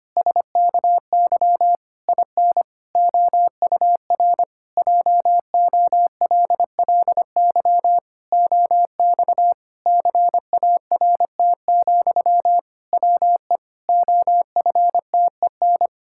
Morse code, heard as a keyed tone, about 25 wpm, 700 Hz. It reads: SKY IN OUR JOLLY OX CART, WE OFTEN
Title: Pangram Challenge v2 - 25wpm